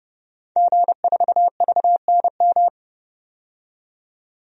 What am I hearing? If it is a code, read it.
Morse code, heard as a keyed tone, about 30 wpm, 700 Hz. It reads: G4VNM
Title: DXCC-100 Call Signs - 5 Characters - 30wpm